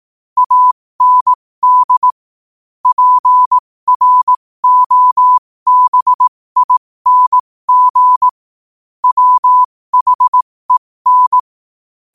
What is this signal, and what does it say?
Morse code, heard as a keyed tone, about 18 wpm, spaced as for 15 wpm, 1000 Hz. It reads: AND PROBING WHEN
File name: qotdfiles / 15wpm